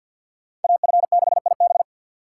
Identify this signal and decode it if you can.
Morse code, heard as a keyed tone, about 52 wpm, 700 Hz. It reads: AF6IB
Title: Speed Racing - USA 2x2 Call Signs 50wpm